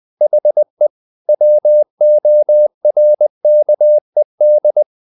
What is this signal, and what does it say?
Morse code, heard as a keyed tone, about 20 wpm, 600 Hz. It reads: HE WORKED
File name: the old man and the sea part 2 20wpm